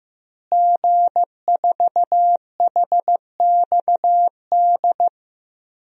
Morse code, heard as a keyed tone, about 15 wpm, 700 Hz. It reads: G4HXD
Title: DXCC-100 Call Signs - 5 Characters - Repeated 2x - 15wpm